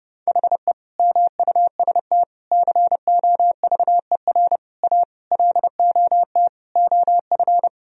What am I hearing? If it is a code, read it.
Morse code, heard as a keyed tone, about 30 wpm, 700 Hz. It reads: HE MUST COVER A LOT OF
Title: Sentences from Top 1000 Words - Review Part 3 - 30wpm